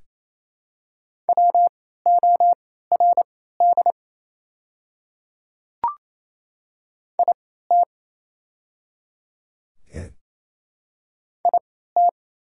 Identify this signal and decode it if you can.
Morse code, heard as a keyed tone, about 28 wpm, spaced as for 16 wpm, 690 Hz. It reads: WORD IT IT
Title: 3x Character Spacing - Top 100 Words - 28wpm